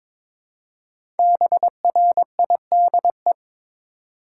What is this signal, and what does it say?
Morse code, heard as a keyed tone, about 22 wpm, 700 Hz. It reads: BRIDE
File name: 5 Letter Words - First 3 Letters Given - 22wpm